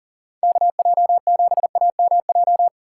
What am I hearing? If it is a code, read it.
Morse code, heard as a keyed tone, about 39 wpm, 700 Hz. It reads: KJ7AMJ
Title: USA Call Signs - 6 Characters - 4x Character Spacing - 40wpm